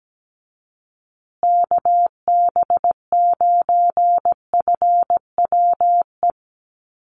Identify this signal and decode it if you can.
Morse code, heard as a keyed tone, about 17 wpm, 700 Hz. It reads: KB9FWE
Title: USA Call Signs Repeated 2x 17wpm